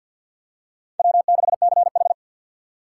Morse code, heard as a keyed tone, about 52 wpm, 700 Hz. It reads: W6CH
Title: USA Call Signs - 4x Character Spacing - 50wpm